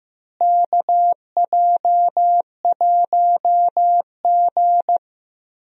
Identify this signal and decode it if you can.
Morse code, heard as a keyed tone, about 15 wpm, 700 Hz. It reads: KJ1G
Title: USA Call Signs - 4 Characters - Repeated 3x - 15wpm